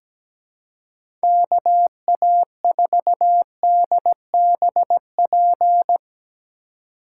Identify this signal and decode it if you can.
Morse code, heard as a keyed tone, about 17 wpm, 700 Hz. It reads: KA4DBP
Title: USA Call Signs - 6 Characters - 4x Character Spacing - 17wpm